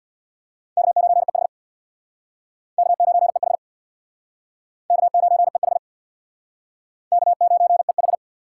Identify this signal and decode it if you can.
Morse code, heard as a keyed tone, about 66 wpm, 700 Hz. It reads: K9EH K9EH K9EH K9EH